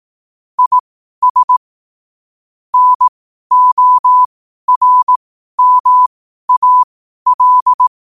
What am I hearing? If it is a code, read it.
Morse code, heard as a keyed tone, about 18 wpm, spaced as for 12 wpm, 1000 Hz. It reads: IS NORMAL
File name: qotdfiles / 12wpm